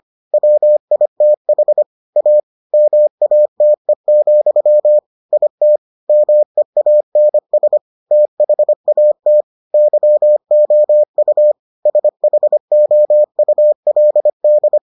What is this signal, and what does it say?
Morse code, heard as a keyed tone, about 25 wpm, 600 Hz. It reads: WITH A MATE, IT MEANS THAT YOU SHOULD